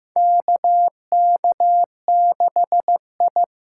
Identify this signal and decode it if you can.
Morse code, heard as a keyed tone, about 15 wpm, 700 Hz. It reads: KK6I
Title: USA Call Signs - 6 Characters - 8x Character Spacing - 15wpm